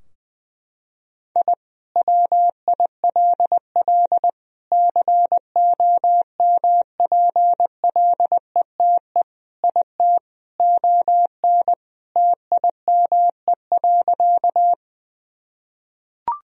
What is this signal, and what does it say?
Morse code, heard as a keyed tone, about 21 wpm, 700 Hz. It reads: I WILL COMPLETE IT ON TIME.
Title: Sentences from Top 1000 Words - Review Part 2 - 20wpm